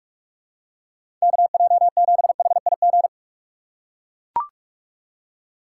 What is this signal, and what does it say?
Morse code, heard as a keyed tone, about 41 wpm, 700 Hz. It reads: KJ7HIG
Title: USA Call Signs - 6 Characters - 4x Character Spacing - 45wpm